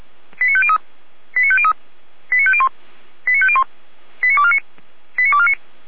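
Five-tone selective call (ZVEI), heard as quick runs of five tones, tones of about 70 ms. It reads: 89503 89503 89501 89501 89249 89249
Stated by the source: ZVEI five-tone selective call received by SDR.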